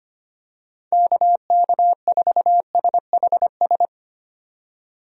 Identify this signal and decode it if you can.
Morse code, heard as a keyed tone, about 25 wpm, 700 Hz. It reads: KK4SHS
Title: USA Call Signs - 6 Characters - 25wpm